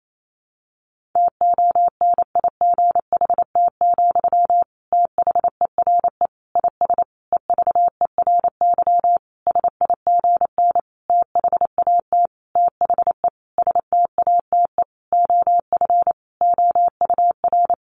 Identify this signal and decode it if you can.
Morse code, heard as a keyed tone, about 28 wpm, 700 Hz. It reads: TONIGHT, THERE IS EVERY SIGN THAT THE STATE OF OUR
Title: Sentences from Top 1400 Words 28wpm